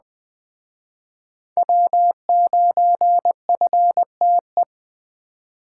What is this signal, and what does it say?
Morse code, heard as a keyed tone, about 21 wpm, 700 Hz. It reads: W9FTE